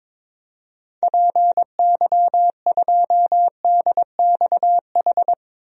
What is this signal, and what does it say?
Morse code, heard as a keyed tone, about 22 wpm, 700 Hz. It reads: PY2DXH
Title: DXCC-100 Call Signs - 6 Characters - Repeated 3x - 22wpm